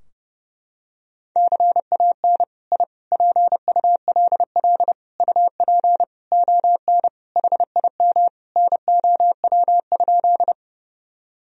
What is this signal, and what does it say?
Morse code, heard as a keyed tone, about 30 wpm, 700 Hz. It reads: CAN I PULL UP ON HIM NOW?
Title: Sentences from Top 1000 Words - Review Part 2 - 30wpm